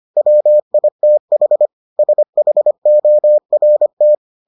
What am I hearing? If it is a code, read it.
Morse code, heard as a keyed tone, about 25 wpm, 600 Hz. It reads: WITH SHORT